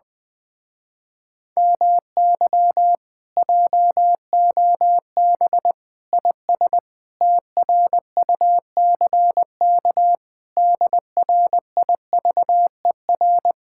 Morse code, heard as a keyed tone, about 19 wpm, 700 Hz. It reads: MY JOB IS TRUCK DRIVER
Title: Speed-Racing Common QSO Phrases - 20wpm